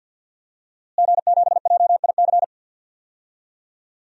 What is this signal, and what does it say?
Morse code, heard as a keyed tone, about 47 wpm, 700 Hz. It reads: G7JIC
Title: DXCC-100 Call Signs - 5 Characters - Repeated 2x - 50wpm